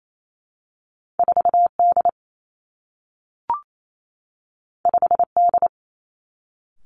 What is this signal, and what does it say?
Morse code, heard as a keyed tone, about 28 wpm, 700 Hz. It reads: 4 D 5 D
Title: Two Challenging Letter-Number Combinations 28wpm